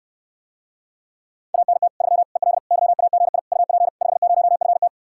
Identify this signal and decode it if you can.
Morse code, heard as a keyed tone, about 69 wpm, 700 Hz. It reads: ANT 3 EL YAGI UP 50FT